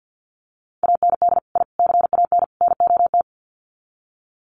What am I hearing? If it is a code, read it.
Morse code, heard as a keyed tone, about 50 wpm, 700 Hz. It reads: AND I CAN NOT
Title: Sets of 4 Words 50wpm